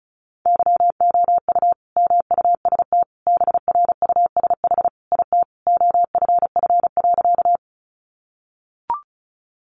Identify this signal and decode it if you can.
Morse code, heard as a keyed tone, about 34 wpm, 700 Hz. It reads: YOU MUST BRUSH IT OFF.
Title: Sentences from Top 1100 Words 35wpm